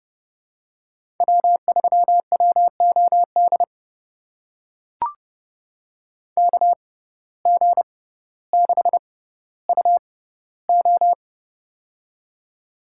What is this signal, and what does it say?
Morse code, heard as a keyed tone, about 32 wpm, spaced as for 16 wpm, 700 Hz. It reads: W3WOD KG6UO
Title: USA Call Signs - 5 Characters - 6x Character Spacing - 30wpm